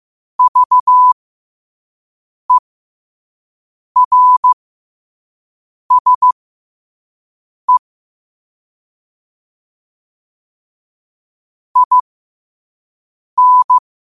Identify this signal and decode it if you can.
Morse code, heard as a keyed tone, about 15 wpm, spaced as for 5 wpm, 1000 Hz. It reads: VERSE IN